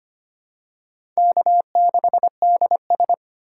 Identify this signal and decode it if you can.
Morse code, heard as a keyed tone, about 25 wpm, 700 Hz. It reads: K6DS
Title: USA Call Signs - 4 Characters - 4x Character Spacing - 25wpm